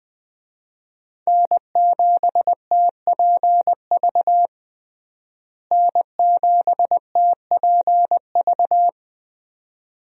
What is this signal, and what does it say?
Morse code, heard as a keyed tone, about 21 wpm, 700 Hz. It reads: N7TPV N7TPV
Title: USA Call Signs - 5 Characters - Repeated 2x - 20wpm